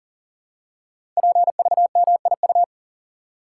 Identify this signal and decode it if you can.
Morse code, heard as a keyed tone, about 42 wpm, 700 Hz. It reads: PVMIU
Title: Code Groups - 5 Letter:Numbers - 40wpm